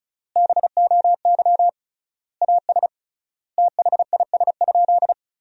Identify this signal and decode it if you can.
Morse code, heard as a keyed tone, about 36 wpm, 700 Hz. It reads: BOY AS THIS?
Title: 3x Word-Spacing - Sentences from Top 400 Words - 35wpm